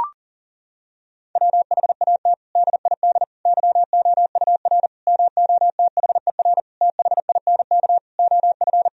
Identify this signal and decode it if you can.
Morse code, heard as a keyed tone, about 40 wpm, 700 Hz. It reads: WHAT DID YOUR MOTHER THINK OF
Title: Sentences from Top 1000 Words - Review Part 4 - 40wpm